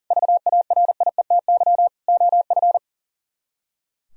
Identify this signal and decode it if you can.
Morse code, heard as a keyed tone, about 51 wpm, 700 Hz. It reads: VARIETY OF